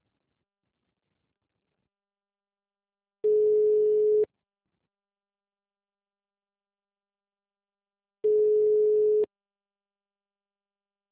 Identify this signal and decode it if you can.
Ringback tone (Europe, 425 Hz).
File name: out- -2033- - - .56393